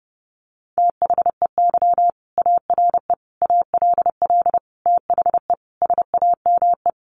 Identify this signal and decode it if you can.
Morse code, heard as a keyed tone, about 30 wpm, 700 Hz. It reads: THEY ARE ALL THE SAME
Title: Sentences from Top 600 Words 30wpm